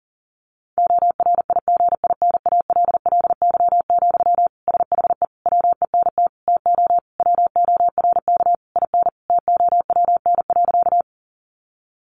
Morse code, heard as a keyed tone, about 40 wpm, 700 Hz. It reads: ORIGINALLY, SHE WENT TO WORK IN TOWN.
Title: Sentences from Top 1500 Words 40wpm